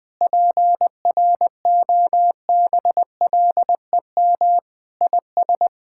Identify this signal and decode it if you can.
Morse code, heard as a keyed tone, about 19 wpm, 700 Hz. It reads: PROBLEM IS
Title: Sentences with 3-Word Phrases - Spoken Beginning - 20wpm